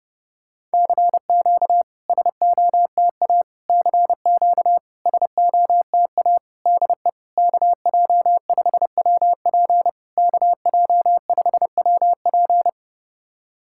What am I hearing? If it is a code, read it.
Morse code, heard as a keyed tone, about 30 wpm, 660 Hz. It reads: CQ SOTA CQ SOTA DE KJ5WP KJ5WP
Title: SOTA QSO Practice 30wpm